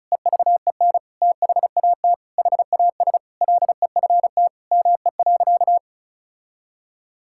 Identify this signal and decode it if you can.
Morse code, heard as a keyed tone, about 36 wpm, 700 Hz. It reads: EVEN THAT HAS LEFT ME.
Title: Sentences from Top 1600 Words 35wpm